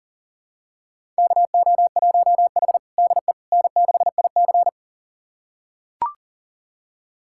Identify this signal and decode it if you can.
Morse code, heard as a keyed tone, about 40 wpm, 700 Hz. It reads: KO1H DE N6IC
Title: USA Callsign DE Callsign - 4 Characters - 40wpm